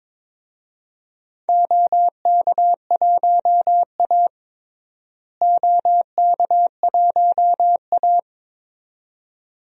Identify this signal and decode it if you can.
Morse code, heard as a keyed tone, about 22 wpm, 700 Hz. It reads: OK1A OK1A
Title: DXCC-100 Call Signs - 4 Characters - Repeated 2x - 22wpm